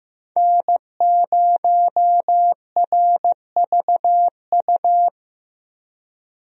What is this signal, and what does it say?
Morse code, heard as a keyed tone, about 16 wpm, 700 Hz. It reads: N0RVU
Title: USA Call Signs - 5 Characters - Repeated 3x - 15wpm